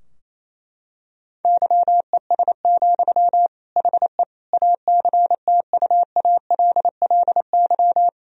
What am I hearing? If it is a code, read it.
Morse code, heard as a keyed tone, about 27 wpm, 700 Hz. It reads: YES, HE ACTUALLY
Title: Sentences from Top 1000 Words - Review Part 4 - 28wpm